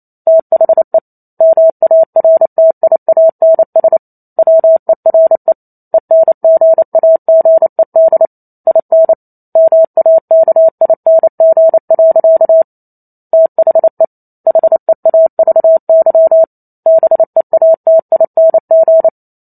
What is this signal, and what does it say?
Morse code, heard as a keyed tone, about 28 wpm, 650 Hz. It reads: THE MARTIANS WERE ENGAGED IN MAKING. THE HEAVY BEATING
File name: Chapter 2 - 28 WPM